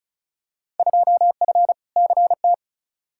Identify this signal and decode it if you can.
Morse code, heard as a keyed tone, about 34 wpm, 700 Hz. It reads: 2F CT
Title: Field Day Exchanges Encore - 35wpm